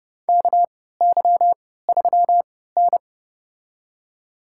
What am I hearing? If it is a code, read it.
Morse code, heard as a keyed tone, about 30 wpm, spaced as for 17 wpm, 700 Hz. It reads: KY3N
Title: USA Call Signs - 4 Characters - 3x Character Spacing - 30wpm